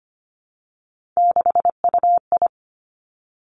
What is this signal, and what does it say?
Morse code, heard as a keyed tone, about 25 wpm, 700 Hz. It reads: 6UI
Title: Three Challenging Letter-Number Combinations 25wpm